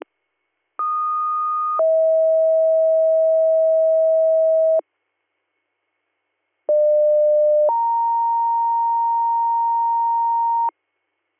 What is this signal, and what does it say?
Two-tone page, heard as a one-second tone, then a three-second tone, about 1230 then 640 Hz, about 600 then 930 Hz. It is heard from 0.8 to 10.7 s.